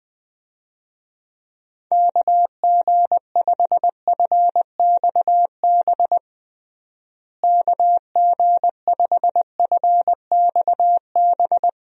Morse code, heard as a keyed tone, about 20 wpm, 700 Hz. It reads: KG5FXB KG5FXB
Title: USA Call Signs - 6 Characters - Repeated 3x - 20wpm